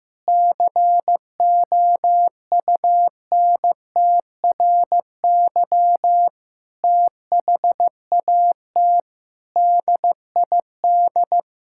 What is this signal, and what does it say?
Morse code, heard as a keyed tone, about 15 wpm, 700 Hz. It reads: COUNTRY THAT DID
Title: Sentences from Top 1600 Words 15wpm